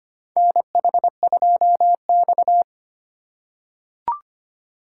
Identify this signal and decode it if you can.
Morse code, heard as a keyed tone, about 25 wpm, 700 Hz. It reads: NH2X
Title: USA Call Signs - 4 Characters - 4x Character Spacing - 25wpm